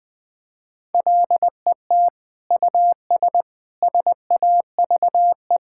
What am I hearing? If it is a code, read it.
Morse code, heard as a keyed tone, about 20 wpm, 700 Hz. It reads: LET US SAVE
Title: Sentences from Top 1600 Words 20wpm